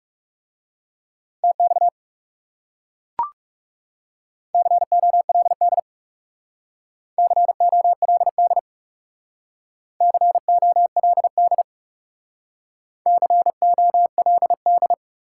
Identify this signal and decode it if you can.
Morse code, heard as a keyed tone, about 40 wpm, 700 Hz. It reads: TX COLD COLD COLD COLD